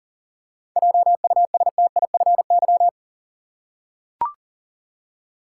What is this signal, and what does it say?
Morse code, heard as a keyed tone, about 39 wpm, 700 Hz. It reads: JUSTIFY
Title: Seven Letter Words - 40wpm